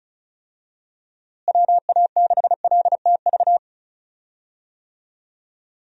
Morse code, heard as a keyed tone, about 36 wpm, 700 Hz. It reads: WA6LTV
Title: USA Call Signs - 6 Characters - Repeated 3x - 35wpm